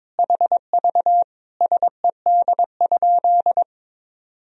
Morse code, heard as a keyed tone, about 22 wpm, 700 Hz. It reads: HV SED?
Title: Sentences from Top 100 Words and Abbreviations 22wpm